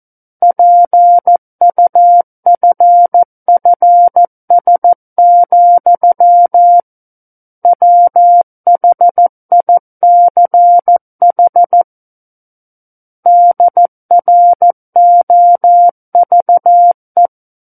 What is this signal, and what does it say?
Morse code, heard as a keyed tone, about 14 wpm, 700 Hz. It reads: PUFFS, WHICH DROVE
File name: Book One / 05-Chapter 5 - 14 WPM